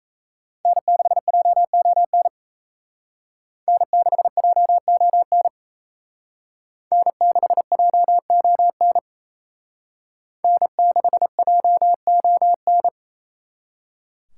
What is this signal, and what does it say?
Morse code, heard as a keyed tone, about 34 wpm, 700 Hz. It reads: N6JON N6JON N6JON N6JON
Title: Speed Racing - USA Call Signs 28wpm